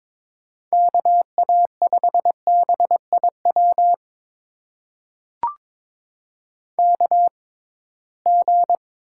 Morse code, heard as a keyed tone, about 22 wpm, spaced as for 17 wpm, 700 Hz. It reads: KA5BIW KG
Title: USA Call Signs - 6 Characters - 6x Character Spacing - 22wpm